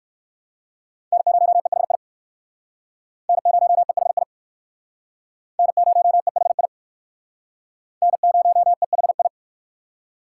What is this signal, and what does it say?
Morse code, heard as a keyed tone, about 59 wpm, 690 Hz. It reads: N0EHI N0EHI N0EHI N0EHI